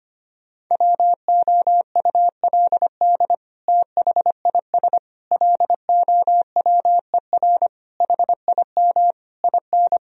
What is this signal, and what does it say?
Morse code, heard as a keyed tone, about 25 wpm, 700 Hz. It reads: WOULD THIS LOWER HIM IN
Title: Sentences from Top 1000 Words - Review Part 4 - 25wpm